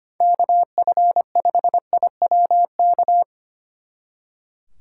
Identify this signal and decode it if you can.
Morse code, heard as a keyed tone, about 29 wpm, 700 Hz. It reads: KF5IWK
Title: Speed Racing - USA 2x3 Call Signs 25wpm